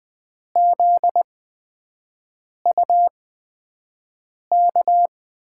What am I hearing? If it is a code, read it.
Morse code, heard as a keyed tone, about 20 wpm, spaced as for 5 wpm, 700 Hz. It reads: ZUK
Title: USA Call Signs - 6 Characters - 8x Character Spacing - 20wpm